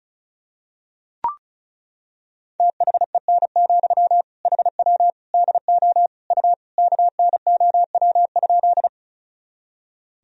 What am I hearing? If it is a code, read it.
Morse code, heard as a keyed tone, about 36 wpm, 700 Hz. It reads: THEN, HW DO U KNOW?
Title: Sentences from Top 100 Words and Abbreviations 35wpm